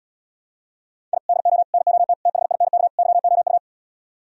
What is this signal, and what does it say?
Morse code, heard as a keyed tone, about 77 wpm, 700 Hz. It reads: I DO NOT THINK YOU